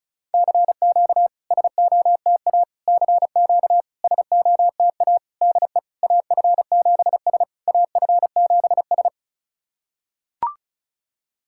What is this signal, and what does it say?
Morse code, heard as a keyed tone, about 36 wpm, 690 Hz. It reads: CQ SOTA CQ SOTA DE AF7S AF7S
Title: SOTA QSO Practice 35wpm